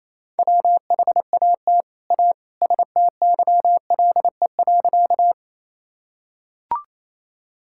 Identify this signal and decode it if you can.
Morse code, heard as a keyed tone, about 27 wpm, 700 Hz. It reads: WHAT A STYLE.
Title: Sentences from Top 1000 Words - Review Part 5 - 28wpm